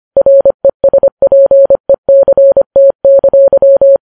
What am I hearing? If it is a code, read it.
Morse code, heard as a keyed tone, about 25 wpm, 550 Hz. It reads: RESPECT!